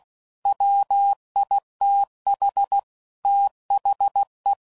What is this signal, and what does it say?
Morse code, heard as a keyed tone, about 16 wpm, 800 Hz. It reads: WITH THE
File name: CW Code Groups @15WPM / Technical Subjects / XC